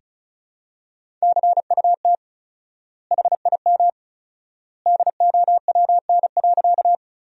Morse code, heard as a keyed tone, about 36 wpm, 700 Hz. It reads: CUT HIM DOWN.
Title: 4x Word-Spacing - Sentences from Top 300 Words - 35wpm